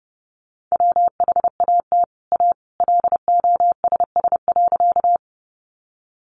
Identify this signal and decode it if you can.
Morse code, heard as a keyed tone, about 30 wpm, 700 Hz. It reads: WHAT A LOSS.
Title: Sentences from Top 1400 Words 30wpm